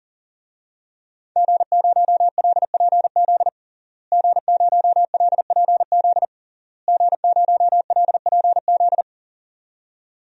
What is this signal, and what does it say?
Morse code, heard as a keyed tone, about 40 wpm, 700 Hz. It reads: G0LPZ G0LPZ G0LPZ
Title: DXCC-100 Call Signs - 5 Characters - Repeated 3x - 40wpm